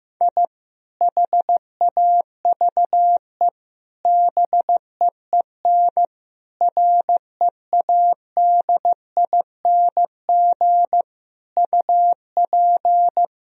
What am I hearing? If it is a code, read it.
Morse code, heard as a keyed tone, about 15 wpm, 700 Hz. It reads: I HAVE BEEN READING UP
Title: Sentences from Top 1000 Words - Review Part 3 - 15wpm